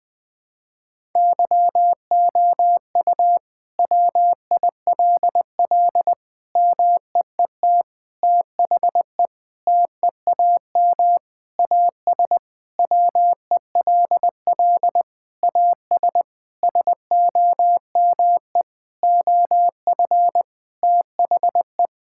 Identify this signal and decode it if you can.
Morse code, heard as a keyed tone, about 20 wpm, 700 Hz. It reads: YOU WILL MEET THE TEAM AS WELL AS SOME OF THE
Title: Sentences with 6-Word Phrases - Spoken Beginning - 20wpm